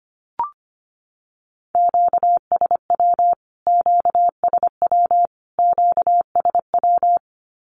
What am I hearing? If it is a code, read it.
Morse code, heard as a keyed tone, about 25 wpm, 700 Hz. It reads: QSW QSW QSW
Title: Q-Codes Repeated 3x 25wpm